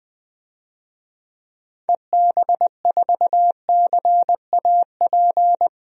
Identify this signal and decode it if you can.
Morse code, heard as a keyed tone, about 20 wpm, 700 Hz. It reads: EB4CAP
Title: DXCC-100 Call Signs - 6 Characters - 20wpm